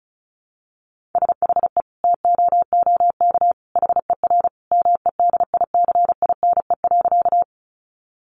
Morse code, heard as a keyed tone, about 34 wpm, 700 Hz. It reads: SHE TOOK HER MEDICINE.
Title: Sentences from Top 900 Words 35wpm